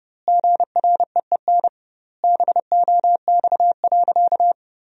Morse code, heard as a keyed tone, about 28 wpm, 700 Hz. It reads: GREEN BOX.
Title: 2x Word-Spacing - Sentences from Top 500 Words - 30wpm